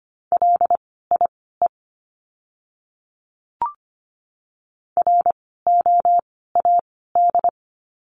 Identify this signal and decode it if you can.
Morse code, heard as a keyed tone, about 25 wpm, spaced as for 16 wpm, 700 Hz. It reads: LIE ROAD
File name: course / 25:10wpm / 094 Lesson 16f - Letter c - Words 25wpm